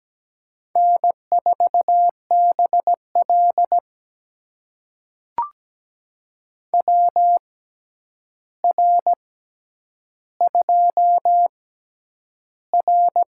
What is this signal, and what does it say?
Morse code, heard as a keyed tone, about 17 wpm, spaced as for 9 wpm, 700 Hz. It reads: N4BL WR2R
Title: USA Call Signs - 4 Characters - 6x Character Spacing - 17wpm